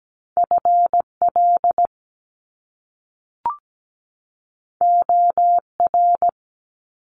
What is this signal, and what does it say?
Morse code, heard as a keyed tone, about 17 wpm, 700 Hz. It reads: FL OR
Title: US States 17wpm